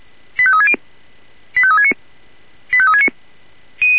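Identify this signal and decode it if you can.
Five-tone selective call (ZVEI), heard as quick runs of five tones, tones of about 70 ms. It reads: 85360 85369 85369
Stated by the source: ZVEI five-tone selective call 85355, received by SDR.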